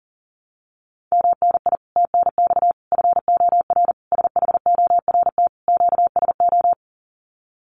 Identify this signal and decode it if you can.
Morse code, heard as a keyed tone, about 40 wpm, 700 Hz. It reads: MNI TNX FOR SHORT QSO
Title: Common QSO Phrases 40wpm